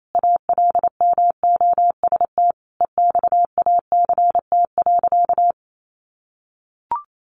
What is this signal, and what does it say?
Morse code, heard as a keyed tone, about 28 wpm, 700 Hz. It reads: ALMOST EXACT.
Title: Sentences from Top 1300 Words 28wpm